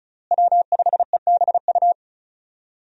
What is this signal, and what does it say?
Morse code, heard as a keyed tone, about 34 wpm, 700 Hz. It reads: W5EBU
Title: USA Call Signs - 5 Characters - Repeated 2x - 35wpm